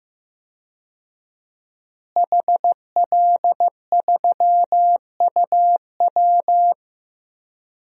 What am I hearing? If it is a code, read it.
Morse code, heard as a keyed tone, about 15 wpm, 700 Hz. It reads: HL3UW
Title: DXCC-100 Call Signs - 5 Characters - Repeated 2x - 15wpm